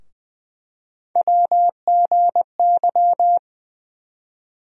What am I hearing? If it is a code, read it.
Morse code, heard as a keyed tone, about 20 wpm, 700 Hz. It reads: WGY
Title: USA Call Sign Postfixes - 3 Letters 20wpm